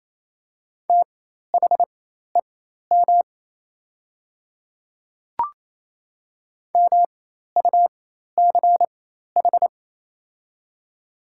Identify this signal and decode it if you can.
Morse code, heard as a keyed tone, about 28 wpm, spaced as for 13 wpm, 700 Hz. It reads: THEM MUCH
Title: 4x Character Spacing - Top 200 Words - 28wpm